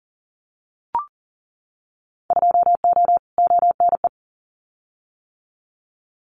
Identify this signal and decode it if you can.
Morse code, heard as a keyed tone, about 40 wpm, 700 Hz. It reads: 2O ONE
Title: Winter Field Day Exchanges Encore 40wpm